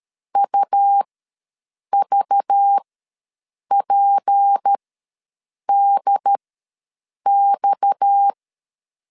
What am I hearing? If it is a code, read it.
Morse code, heard as a keyed tone, about 13 wpm, spaced as for 7 wpm, 800 Hz. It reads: UVPDX
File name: LEZIONI / 6 PKX0 / 6 PKX0 ws 12wpm 3448